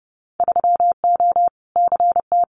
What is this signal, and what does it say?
Morse code, heard as a keyed tone, about 30 wpm, 700 Hz. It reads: 3O CT
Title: Winter Field Day Exchanges 30wpm